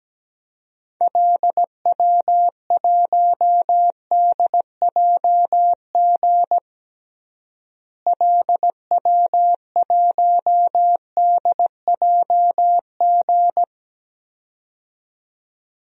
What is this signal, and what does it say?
Morse code, heard as a keyed tone, about 17 wpm, 700 Hz. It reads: LW1DJG LW1DJG
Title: DXCC-100 Call Signs - 6 Characters - Repeated 2x - 17wpm